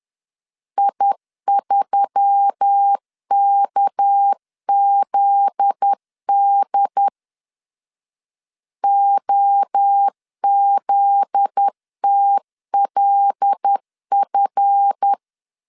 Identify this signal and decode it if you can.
Morse code, heard as a keyed tone, about 10 wpm, 800 Hz. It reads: I3KZD OZTLF
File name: LEZIONI / 8 YJ45 / 8 Y 10wpm 2615